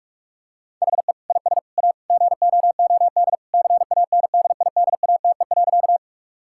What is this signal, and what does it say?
Morse code, heard as a keyed tone, about 41 wpm, 700 Hz. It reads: HE IS A GOOD CANDIDATE.
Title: Sentences from Top 1600 Words 45wpm